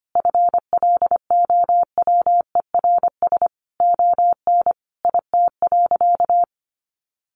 Morse code, heard as a keyed tone, about 25 wpm, 700 Hz. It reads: FLOWERS ON IT.
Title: Additional Easy Phrases 25wpm